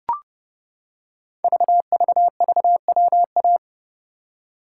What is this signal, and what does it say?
Morse code, heard as a keyed tone, about 30 wpm, 700 Hz. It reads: VVVWA
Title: Code Groups - 5 Letters - 30wpm